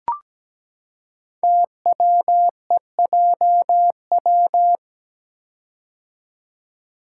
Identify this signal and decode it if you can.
Morse code, heard as a keyed tone, about 17 wpm, 700 Hz. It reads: TWEJW
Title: Code Groups - 5 Letters - 17wpm